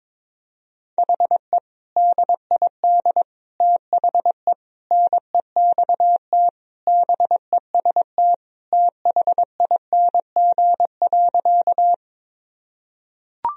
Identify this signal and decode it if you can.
Morse code, heard as a keyed tone, about 22 wpm, 700 Hz. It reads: HE DID THE NEXT BEST THING.
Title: Sentences from Top 1000 Words - Review Part 1 - 22wpm